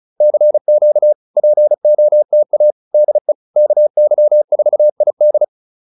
Morse code, heard as a keyed tone, about 35 wpm, 600 Hz. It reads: CQ POTA DE KY4ID